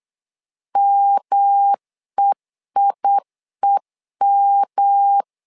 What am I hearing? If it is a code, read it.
Morse code, heard as a keyed tone, about 8 wpm, 800 Hz. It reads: MEIEM
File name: LEZIONI / 3 MWRO / 3 M 8wpm 2300